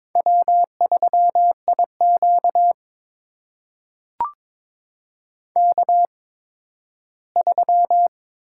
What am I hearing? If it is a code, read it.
Morse code, heard as a keyed tone, about 22 wpm, spaced as for 13 wpm, 700 Hz. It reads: W3IQ K3
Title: USA Call Signs - 4 Characters - 8x Character Spacing - 22wpm